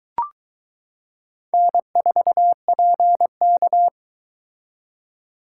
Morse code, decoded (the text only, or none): N4PK